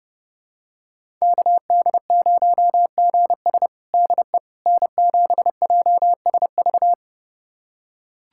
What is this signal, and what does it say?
Morse code, heard as a keyed tone, about 30 wpm, 700 Hz. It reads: KD0GS DE N7JSV
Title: USA Callsign DE Callsign - 5 Characters - 30wpm